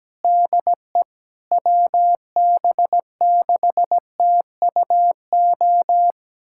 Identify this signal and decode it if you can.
Morse code, heard as a keyed tone, about 17 wpm, 700 Hz. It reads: DE WB6TUO
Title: USA Callsign DE Callsign 17wpm